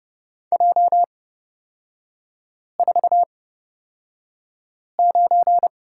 Morse code, heard as a keyed tone, about 30 wpm, spaced as for 5 wpm, 700 Hz. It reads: J49
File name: Lesson 37e - Letter x - Three Characters 30wpm